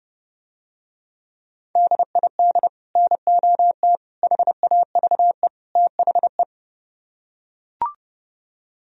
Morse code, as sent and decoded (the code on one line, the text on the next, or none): -.. .. -.. / -. --- - / .... .- ...- . / - .... .
DID NOT HAVE THE